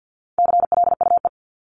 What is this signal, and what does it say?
Morse code, heard as a keyed tone, about 50 wpm, 700 Hz. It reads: CLUE
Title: Top 1751-2000 Words 50wpm